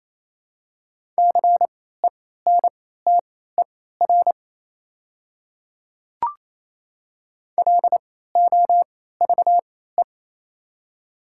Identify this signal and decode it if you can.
Morse code, heard as a keyed tone, about 27 wpm, spaced as for 16 wpm, 700 Hz. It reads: CENTER LOVE
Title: 3x Character Spacing - Top 500 Words - 28wpm